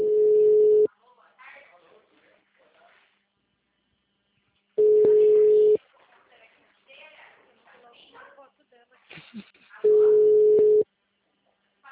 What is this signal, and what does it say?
Ringback tone (Europe, 425 Hz), with speech.